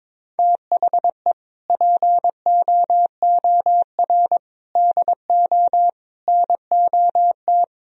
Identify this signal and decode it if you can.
Morse code, heard as a keyed tone, about 22 wpm, 700 Hz. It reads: THE POOR DO NOT
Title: Sentences from Top 1000 Words - Review Part 2 - 22wpm